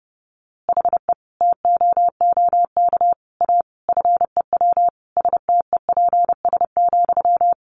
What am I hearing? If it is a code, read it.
Morse code, heard as a keyed tone, about 28 wpm, 700 Hz. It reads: HE TOOK A FEW STEPS,
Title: A Sound Of Thunder 30wpm